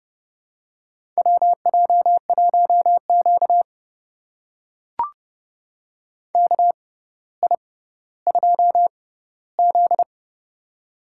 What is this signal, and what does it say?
Morse code, heard as a keyed tone, about 30 wpm, spaced as for 15 wpm, 700 Hz. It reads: WJ1Q KI2Z
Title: USA Call Signs - 4 Characters - 6x Character Spacing - 30wpm